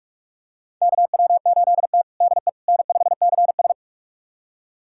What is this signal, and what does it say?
Morse code, heard as a keyed tone, about 47 wpm, 700 Hz. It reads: KW8T DE N5CS
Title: USA Callsign DE Callsign - 4 Characters - 45wpm